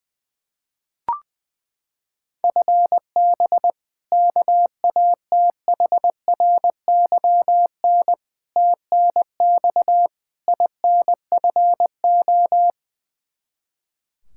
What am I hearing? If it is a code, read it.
Morse code, heard as a keyed tone, about 21 wpm, 700 Hz. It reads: FB KATHRYN TNX INFO
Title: Practice QSO - Exchange 3 - More Info - 20wpm